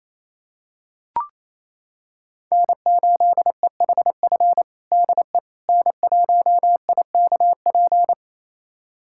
Morse code, heard as a keyed tone, about 28 wpm, 700 Hz. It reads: N8EHF DE N1IKP
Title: USA Callsign DE Callsign - 5 Characters - 28wpm